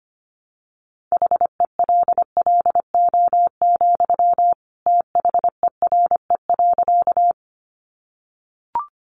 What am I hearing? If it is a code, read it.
Morse code, heard as a keyed tone, about 25 wpm, 700 Hz. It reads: HELLO, THERE.
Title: Sentences from Top 1100 Words 25wpm